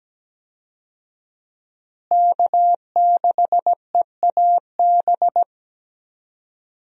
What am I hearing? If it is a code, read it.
Morse code, heard as a keyed tone, about 17 wpm, 700 Hz. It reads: K6EAB
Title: USA Call Signs - 5 Characters - Repeated 2x - 17wpm